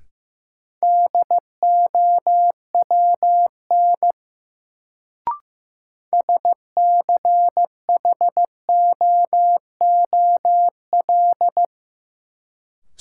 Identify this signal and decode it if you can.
Morse code, heard as a keyed tone, about 15 wpm, 700 Hz. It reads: DOWN SCHOOL
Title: Sets of 2 Words - New Word Drill - 15wpm